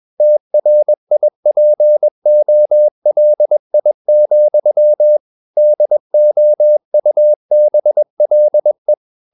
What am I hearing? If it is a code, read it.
Morse code, heard as a keyed tone, about 21 wpm, 600 Hz. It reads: TRIPOLI, DOUBLE
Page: CW med Gnister